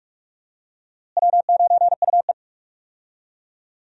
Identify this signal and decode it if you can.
Morse code, heard as a keyed tone, about 45 wpm, 700 Hz. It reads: W9UE
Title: USA Call Signs - 4 Characters - Repeated 3x - 45wpm